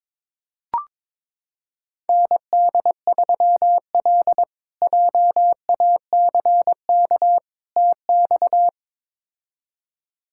Morse code, heard as a keyed tone, about 22 wpm, 700 Hz. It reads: ND3L JACK TX
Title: K1USN SST QSO Practice - 22wpm